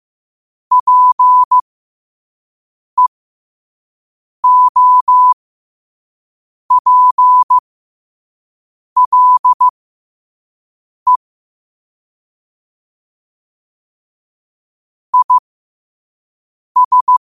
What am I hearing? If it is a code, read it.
Morse code, heard as a keyed tone, about 15 wpm, spaced as for 5 wpm, 1000 Hz. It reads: PEOPLE IS